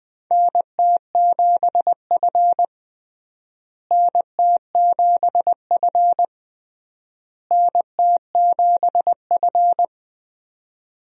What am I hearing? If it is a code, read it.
Morse code, heard as a keyed tone, about 20 wpm, 700 Hz. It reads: NT7F NT7F NT7F
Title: USA Call Signs - 4 Characters - Repeated 3x - 20wpm